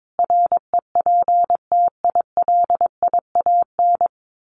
Morse code, heard as a keyed tone, about 22 wpm, 700 Hz. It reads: REPTILIAN
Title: A Sound Of Thunder 22wpm